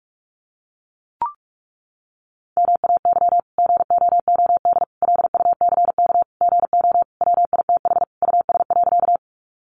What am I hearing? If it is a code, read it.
Morse code, heard as a keyed tone, about 44 wpm, 700 Hz. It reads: MAY GOOD LUCK GO WITH US.
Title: Sentences from Top 1500 Words 45wpm